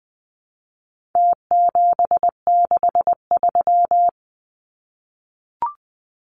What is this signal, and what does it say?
Morse code, heard as a keyed tone, about 20 wpm, 700 Hz. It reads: T763
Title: Numbers - Contest Style - 4-Digit 20wpm